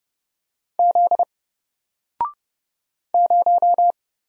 Morse code, heard as a keyed tone, about 30 wpm, 700 Hz. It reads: Z 0
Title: Single Letter-Number Alt 30wpm